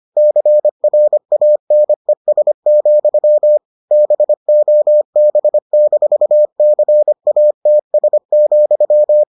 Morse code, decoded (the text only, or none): CRANES, BOB-CATS,